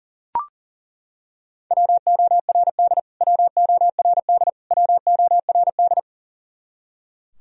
Morse code, heard as a keyed tone, about 40 wpm, 700 Hz. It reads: WORD WORD WORD
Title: Top 100 Words Repeated 40wpm